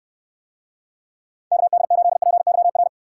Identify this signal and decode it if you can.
Morse code, heard as a keyed tone, about 69 wpm, 700 Hz. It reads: KD8PYR